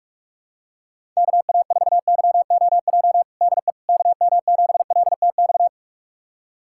Morse code, heard as a keyed tone, about 46 wpm, 690 Hz. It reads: KA4YOJ DE KM7LTX
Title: USA Callsign DE Callsign 45wpm